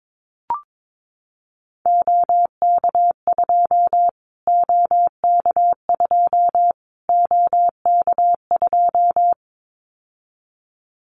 Morse code, heard as a keyed tone, about 22 wpm, 700 Hz. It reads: OK2 OK2 OK2
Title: Callsign Prefixes for Top 25 DXCC - Repeated 3x - 22wpm